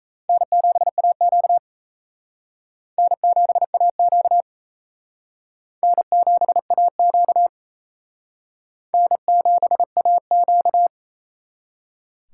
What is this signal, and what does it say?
Morse code, heard as a keyed tone, about 35 wpm, 700 Hz. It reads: N7AQ N7AQ N7AQ N7AQ
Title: Speed Racing - USA 1x2 Call Signs 28wpm